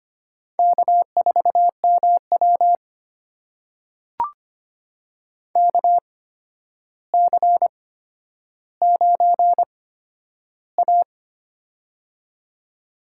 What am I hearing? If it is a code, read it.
Morse code, heard as a keyed tone, about 25 wpm, spaced as for 11 wpm, 700 Hz. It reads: K4MW KC9A
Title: USA Call Signs - 4 Characters - 8x Character Spacing - 25wpm